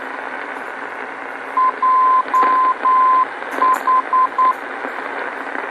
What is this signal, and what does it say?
Morse code, heard as a keyed tone, about 9 wpm, 1030 Hz, with radio noise. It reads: JH
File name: beacon program / tracks / 02 JH Beacon